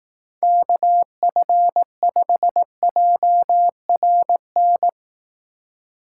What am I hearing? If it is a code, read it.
Morse code, heard as a keyed tone, about 18 wpm, 700 Hz. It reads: KF5JRN
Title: Speed Racing - USA 2x3 Call Signs 15wpm